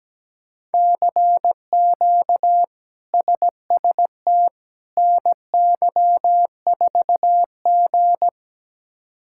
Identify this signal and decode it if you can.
Morse code, heard as a keyed tone, about 17 wpm, 700 Hz. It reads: CQ SST NY4G
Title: K1USN SST QSO Practice - 17wpm